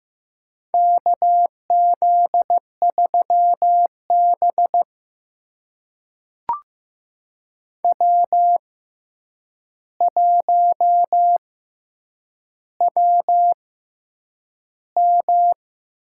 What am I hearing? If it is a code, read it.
Morse code, heard as a keyed tone, about 15 wpm, spaced as for 8 wpm, 700 Hz. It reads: KZ3B W1WM
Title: USA Call Signs - 4 Characters - 6x Character Spacing - 15wpm